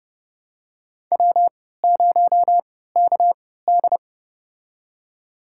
Morse code, heard as a keyed tone, about 30 wpm, spaced as for 17 wpm, 700 Hz. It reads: W0KD
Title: USA Call Signs - 4 Characters - 3x Character Spacing - 30wpm